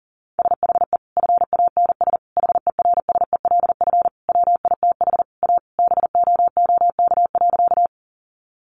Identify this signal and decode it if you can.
Morse code, heard as a keyed tone, about 40 wpm, 700 Hz. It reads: SHE FANS HERSELF WITH A BOOK.
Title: Additional Easy Phrases 40wpm